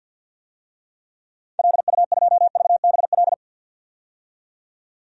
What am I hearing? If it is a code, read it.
Morse code, heard as a keyed tone, about 52 wpm, 700 Hz. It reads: LU2VBL
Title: DXCC-100 Call Signs - 6 Characters - 50wpm